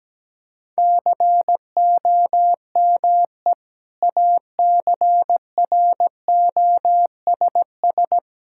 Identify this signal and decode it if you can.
Morse code, heard as a keyed tone, about 17 wpm, 700 Hz. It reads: COME ACROSS
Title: Sentences from Top 1000 Words - Review Part 5 - 17wpm